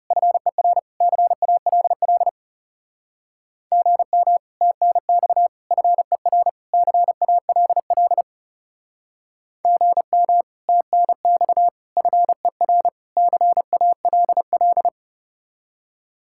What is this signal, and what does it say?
Morse code, heard as a keyed tone, about 40 wpm, 700 Hz. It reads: FER CALL GM TNX FER CALL GM TNX FER CALL